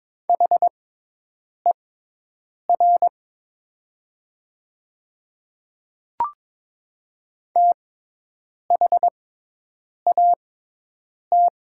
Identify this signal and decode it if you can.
Morse code, heard as a keyed tone, about 22 wpm, spaced as for 8 wpm, 700 Hz. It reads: HER THAT
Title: 6x Character Spacing - Top 100 Words - 22wpm